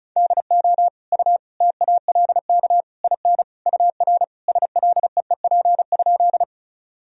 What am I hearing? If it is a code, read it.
Morse code, heard as a keyed tone, about 36 wpm, 700 Hz. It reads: DO U TALK IN UR SLEEP?
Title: Sentences from Top 500 Words and Abbreviations 35wpm